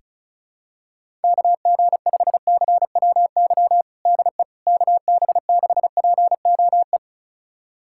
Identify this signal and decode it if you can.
Morse code, heard as a keyed tone, about 36 wpm, 700 Hz. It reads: KG5CWY DE KB6POE
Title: USA Callsign DE Callsign - 6 Characters - 35wpm